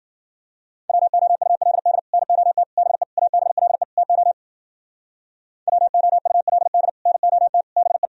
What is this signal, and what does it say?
Morse code, heard as a keyed tone, about 54 wpm, 700 Hz. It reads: WOULD NOT BE ABLE TO WOULD NOT BE